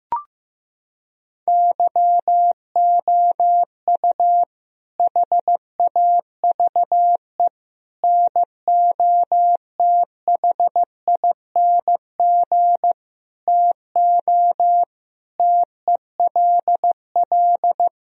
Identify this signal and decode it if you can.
Morse code, heard as a keyed tone, about 15 wpm, 700 Hz. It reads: YOU HAVE NOTHING TO TELL
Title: Sentences from Top 1000 Words - Review Part 4 - 15wpm